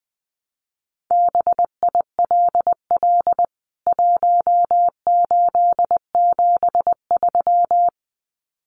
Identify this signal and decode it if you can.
Morse code, heard as a keyed tone, about 20 wpm, 700 Hz. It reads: BILL 1873
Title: CWT Exchange 20wpm